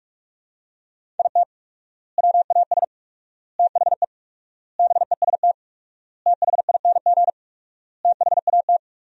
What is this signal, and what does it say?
Morse code, heard as a keyed tone, about 46 wpm, 700 Hz. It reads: IT WAS THE BEST THING THAT
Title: 4x Word-Spacing - Sentences from Top 400 Words - 45wpm